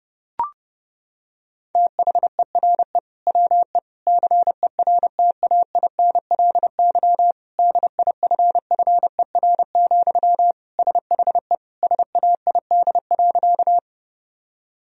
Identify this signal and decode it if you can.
Morse code, heard as a keyed tone, about 30 wpm, 700 Hz. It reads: THERE WE CERTAINLY DIFFER, SHE SAID.
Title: Sentences from Top 1000 Words - Review Part 1 - 30wpm